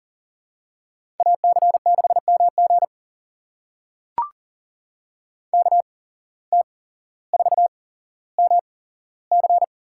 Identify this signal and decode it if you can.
Morse code, heard as a keyed tone, about 39 wpm, spaced as for 17 wpm, 700 Hz. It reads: AC6MG KT4MC
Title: USA Call Signs - 5 Characters - 8x Character Spacing - 40wpm